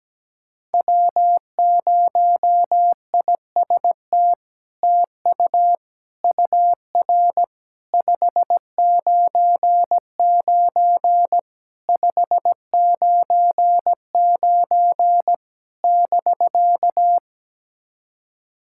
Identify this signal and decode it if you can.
Morse code, heard as a keyed tone, about 17 wpm, 700 Hz. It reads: W0IST TU UR 599 599 <BK>
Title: SOTA QSO Practice 17wpm